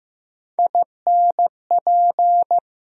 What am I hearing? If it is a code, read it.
Morse code, heard as a keyed tone, about 15 wpm, 700 Hz. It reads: INP
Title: USA Call Sign Postfixes - 3 Letters 15wpm